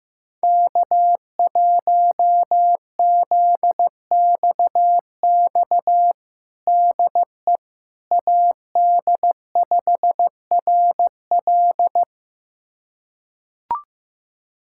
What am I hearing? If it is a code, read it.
Morse code, heard as a keyed tone, about 16 wpm, 700 Hz. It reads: K1ZXX DE AD5RL
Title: USA Callsign DE Callsign - 5 Characters - 15wpm